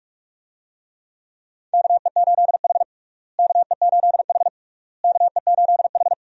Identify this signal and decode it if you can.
Morse code, heard as a keyed tone, about 46 wpm, 700 Hz. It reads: KE8H KE8H KE8H
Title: USA Call Signs - 4 Characters - Repeated 3x - 45wpm